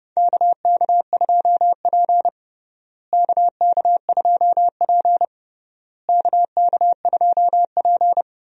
Morse code, heard as a keyed tone, about 29 wpm, 700 Hz. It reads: KK2P KK2P KK2P
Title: USA Call Signs - 4 Characters - Repeated 3x - 30wpm